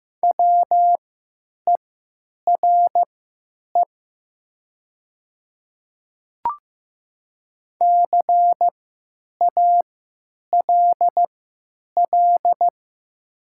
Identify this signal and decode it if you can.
Morse code, heard as a keyed tone, about 15 wpm, spaced as for 9 wpm, 700 Hz. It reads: WERE CALL
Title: 3x Character Spacing - Top 100 Words - 15wpm